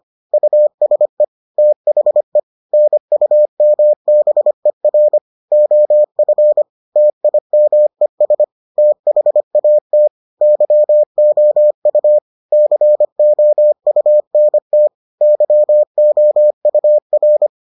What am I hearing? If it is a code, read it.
Morse code, heard as a keyed tone, about 25 wpm, 600 Hz. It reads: USE THE NUMBER OF TIMES THAT YOU COUNT YOUR